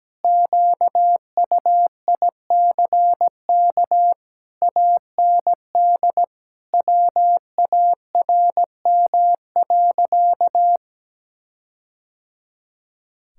Morse code, heard as a keyed tone, about 17 wpm, 700 Hz. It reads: QUICK AND WARM.
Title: Sentences from Top 1600 Words 17wpm